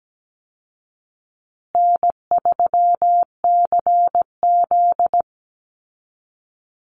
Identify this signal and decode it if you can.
Morse code, heard as a keyed tone, about 17 wpm, 700 Hz. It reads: N3CZ
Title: Call Signs Repeated 2x - 17wpm